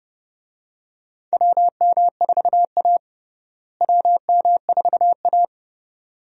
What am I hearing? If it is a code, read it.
Morse code, heard as a keyed tone, about 32 wpm, 700 Hz. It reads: WM4A WM4A
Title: USA Call Signs - 4 Characters - Repeated 3x - 30wpm